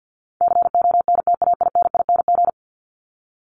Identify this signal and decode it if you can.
Morse code, heard as a keyed tone, about 50 wpm, 700 Hz. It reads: CONTAINING
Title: ING Suffix - Top 100 Words with Base Words 50wpm